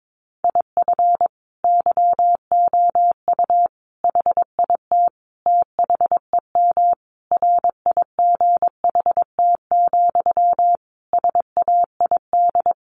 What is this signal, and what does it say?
Morse code, heard as a keyed tone, about 22 wpm, 700 Hz. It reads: IF YOU HIT THEM RIGHT, SAID
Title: A Sound Of Thunder 22wpm